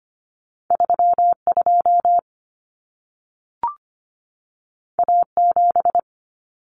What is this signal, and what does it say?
Morse code, heard as a keyed tone, about 25 wpm, 700 Hz. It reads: 32 A7
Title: Common Cut Numbers - 2-Digit 25wpm